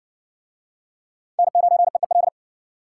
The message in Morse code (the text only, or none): N9EEL